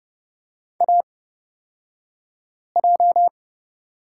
Morse code, heard as a keyed tone, about 30 wpm, spaced as for 5 wpm, 700 Hz. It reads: AJ
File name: Lesson 37d - Letter x - Two Characters 30wpm